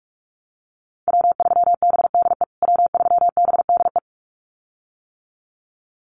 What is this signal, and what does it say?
Morse code, heard as a keyed tone, about 44 wpm, 690 Hz. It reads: W3BDE W3BDE
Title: USA Call Signs Repeated 2x 45wpm